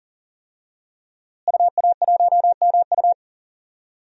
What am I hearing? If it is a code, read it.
Morse code, heard as a keyed tone, about 37 wpm, 700 Hz. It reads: UA1MU
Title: DXCC-100 Call Signs - 5 Characters - Repeated 3x - 40wpm